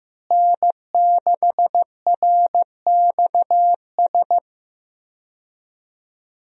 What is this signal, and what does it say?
Morse code, heard as a keyed tone, about 15 wpm, 700 Hz. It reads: N6RXS
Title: USA Call Signs - 5 Characters - Repeated 3x - 15wpm